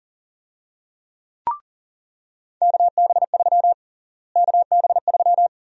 Morse code, heard as a keyed tone, about 40 wpm, 700 Hz. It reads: KB3 KB3
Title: USA Call Sign Prefixes - Repeated 2x - 40wpm